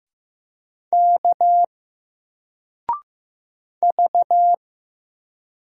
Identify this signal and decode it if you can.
Morse code, heard as a keyed tone, about 15 wpm, 700 Hz. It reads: K V
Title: Single Character (Letter:Number:Punctuation:Prosign) 15wpm